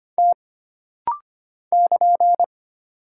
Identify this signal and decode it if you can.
Morse code, heard as a keyed tone, about 25 wpm, 700 Hz. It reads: T <KN>
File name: Single Character (Letter:Number:Punctuation:Prosign) - Rapid-Fire - 25wpm